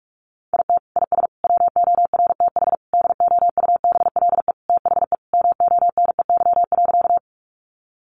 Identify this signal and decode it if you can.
Morse code, heard as a keyed tone, about 44 wpm, 700 Hz. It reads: IT IS WORTH DOUBLE THE MONEY.
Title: Sentences from Top 1400 Words 45wpm